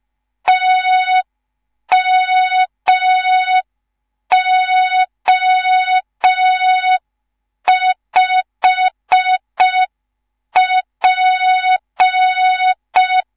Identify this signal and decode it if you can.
Morse code, heard as a keyed tone, about 5 wpm, 3700 Hz. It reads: TMO5P